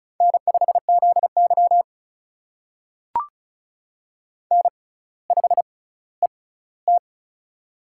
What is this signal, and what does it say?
Morse code, heard as a keyed tone, about 36 wpm, spaced as for 18 wpm, 700 Hz. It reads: N5ZY N5ET
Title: USA Call Signs - 4 Characters - 6x Character Spacing - 35wpm